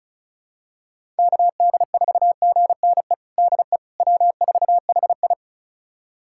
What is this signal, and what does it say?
Morse code, heard as a keyed tone, about 36 wpm, 700 Hz. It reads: KD4GNE DE W4HI
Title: USA Callsign DE Callsign 35wpm